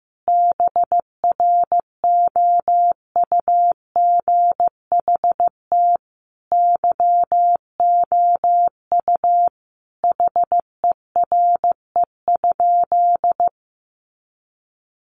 Morse code, heard as a keyed tone, about 15 wpm, 700 Hz. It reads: BROUGHT YOU HERE?
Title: Sentences from Top 500 Words 15wpm